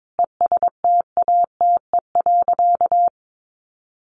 Morse code, heard as a keyed tone, about 22 wpm, 700 Hz. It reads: ESTATE.
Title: Sentences from Top 1200 Words 22wpm